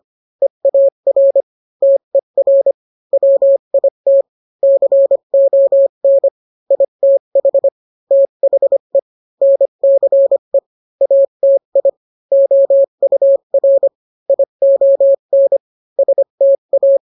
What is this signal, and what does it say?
Morse code, heard as a keyed tone, about 25 wpm, 550 Hz. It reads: EAR TER WIT CON ITH THE NCE ATI OUR ION STA